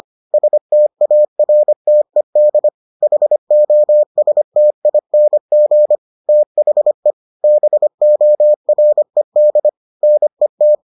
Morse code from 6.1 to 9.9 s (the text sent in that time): THE BORED